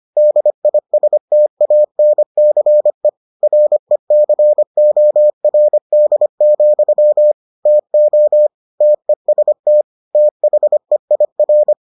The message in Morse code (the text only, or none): DISTANCE RECORD, TO TEST THEIR